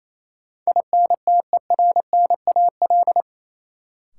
Morse code, heard as a keyed tone, about 28 wpm, 700 Hz. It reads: INTERNAL
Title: Sentences from Top 1400 Words - New Word Drill - 28wpm